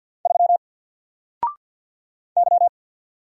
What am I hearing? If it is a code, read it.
Morse code, heard as a keyed tone, about 52 wpm, 700 Hz. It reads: 3 Y
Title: Single Character (Letter:Number:Punctuation:Prosign) Alt 50wpm